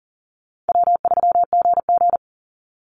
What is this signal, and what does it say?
Morse code, heard as a keyed tone, about 40 wpm, 700 Hz. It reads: W3GG
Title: Easy Call Signs Repeated 2x - 40wpm